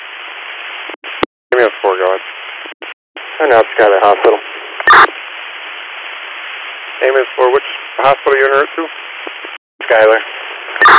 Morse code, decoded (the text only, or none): none